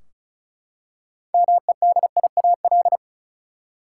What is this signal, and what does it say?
Morse code, heard as a keyed tone, about 36 wpm, 700 Hz. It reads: MEDIAL
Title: K1USN SST QSO Practice 35wpm